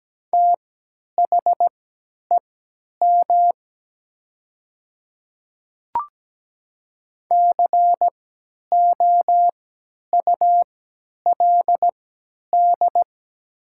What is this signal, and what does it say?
Morse code, heard as a keyed tone, about 17 wpm, spaced as for 10 wpm, 700 Hz. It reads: THEM COULD
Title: 3x Character Spacing - Top 200 Words - 17wpm